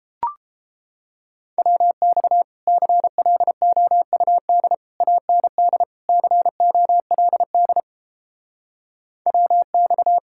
Morse code, heard as a keyed tone, about 31 wpm, 700 Hz. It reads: WX CLOUD AND COLD WX